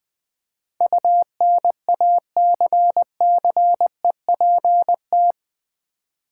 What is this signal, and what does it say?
Morse code, heard as a keyed tone, about 20 wpm, 700 Hz. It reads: UNACCEPT
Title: Build A Word - 4 Letters At A Time - 12-Letter Words - 20wpm